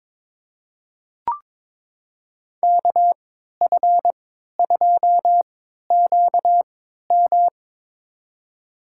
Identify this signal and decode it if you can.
Morse code, heard as a keyed tone, about 22 wpm, spaced as for 13 wpm, 700 Hz. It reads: KF2QM
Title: USA Call Signs - 5 Characters - 3x Character Spacing - 22wpm